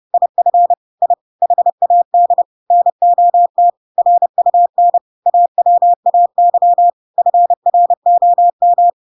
Morse code, heard as a keyed tone, about 30 wpm, 700 Hz. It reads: IF I HAD NOT RUN AWAY FROM
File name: Pinocchio / Pinocchio ch5 30 30 ik1hge